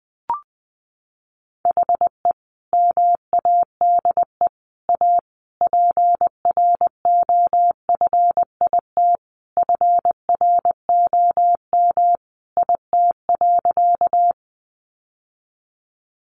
Morse code, heard as a keyed tone, about 20 wpm, 700 Hz. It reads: HE MADE A PROFIT FROM IT.
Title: Sentences from Top 800 Words 20wpm